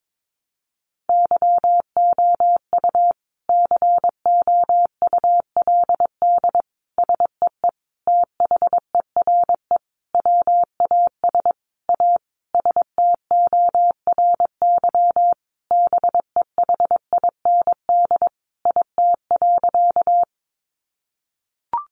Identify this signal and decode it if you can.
Morse code, heard as a keyed tone, about 22 wpm, 700 Hz. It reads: YOU COULD SEE THERE WAS A STORY BEHIND IT.
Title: Sentences from Top 500 Words 22wpm